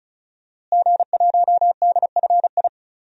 Morse code, heard as a keyed tone, about 34 wpm, 700 Hz. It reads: G1DFI
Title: DXCC-100 Call Signs - 5 Characters - Repeated 3x - 35wpm